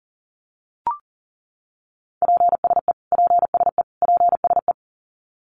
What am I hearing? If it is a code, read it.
Morse code, heard as a keyed tone, about 40 wpm, 700 Hz. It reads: PSE PSE PSE
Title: Abbreviations Repeated 3x - 40wpm